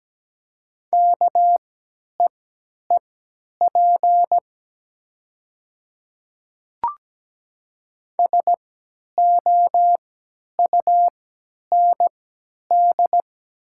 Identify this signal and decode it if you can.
Morse code, heard as a keyed tone, about 17 wpm, spaced as for 10 wpm, 700 Hz. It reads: KEEP SOUND
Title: 3x Character Spacing - Top 300 Words - 17wpm